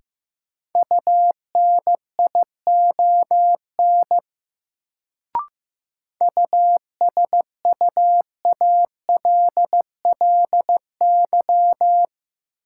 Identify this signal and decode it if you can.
Morse code, heard as a keyed tone, about 15 wpm, 700 Hz. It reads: UNION USUALLY
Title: Sentences from Top 1400 Words - New Word Drill Alt - 15wpm